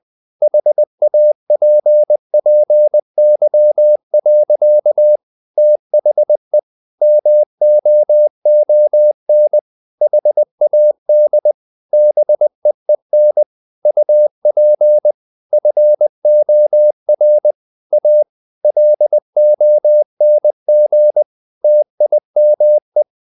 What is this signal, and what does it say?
Morse code, heard as a keyed tone, about 20 wpm, 600 Hz. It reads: HAPPY. THE MOON HAD BEEN UP FOR A LONG TIME
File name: the old man and the sea part 2 20wpm